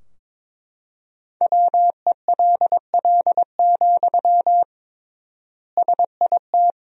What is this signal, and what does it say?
Morse code, heard as a keyed tone, about 22 wpm, 700 Hz. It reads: WELL, SIT
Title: 3x Word-Spacing - Sentences from Top 500 Words - 22wpm